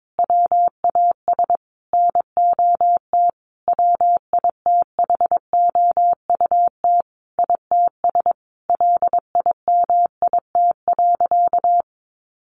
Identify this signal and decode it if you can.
Morse code, heard as a keyed tone, about 22 wpm, 700 Hz. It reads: WAS NOT WITHOUT ITS LIMIT.
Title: Sentences from Top 1500 Words 22wpm